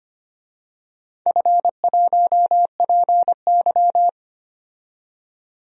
Morse code, heard as a keyed tone, about 25 wpm, 700 Hz. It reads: F1PY
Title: DXCC-100 Call Signs - 4 Characters - 25wpm